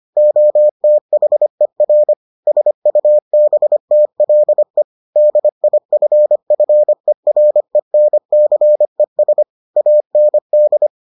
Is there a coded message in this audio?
Yes: Morse code from 0.2 to 10.9 s, heard as a keyed tone, about 25 wpm, 600 Hz. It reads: OTHER SUBTLE DIFFERENCES AND